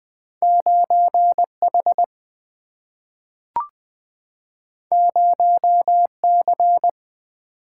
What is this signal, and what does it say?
Morse code, heard as a keyed tone, about 20 wpm, 700 Hz. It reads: 9H 0C
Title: Single-Letter Single-Number 20wpm